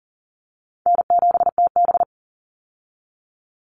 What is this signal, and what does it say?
Morse code, heard as a keyed tone, about 40 wpm, 700 Hz. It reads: N7TB
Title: Call Signs Repeated 3x - 40wpm